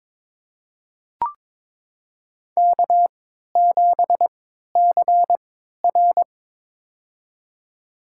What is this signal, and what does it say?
Morse code, heard as a keyed tone, about 22 wpm, spaced as for 12 wpm, 700 Hz. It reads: K7CR
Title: USA Call Signs - 4 Characters - 3x Character Spacing - 22wpm